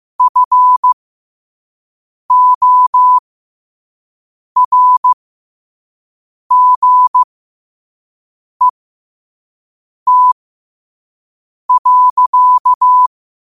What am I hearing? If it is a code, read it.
Morse code, heard as a keyed tone, about 15 wpm, spaced as for 5 wpm, 1000 Hz. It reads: FORGET.
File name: 5wpm